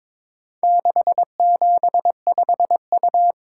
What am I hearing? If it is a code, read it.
Morse code, heard as a keyed tone, about 22 wpm, 700 Hz. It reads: 675U
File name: practice / 22wpm / Four Letter-Number Combinations - 22wpm